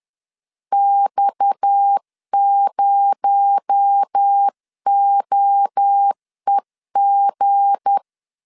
Morse code, heard as a keyed tone, about 10 wpm, 800 Hz. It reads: X0OEG
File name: LEZIONI / 7 GF9Z / 7 G 10wpm 2501